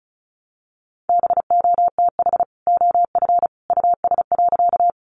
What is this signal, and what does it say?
Morse code, heard as a keyed tone, about 34 wpm, 700 Hz. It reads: BOTH OF US.
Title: Sentences from Top 300 Words 35wpm